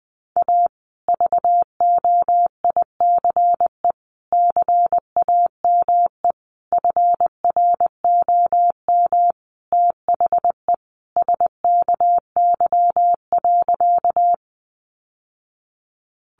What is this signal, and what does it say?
Morse code, heard as a keyed tone, about 20 wpm, 700 Hz. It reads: A VOICE CAME FROM THE SKY.
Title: Sentences from Top 1200 Words 20wpm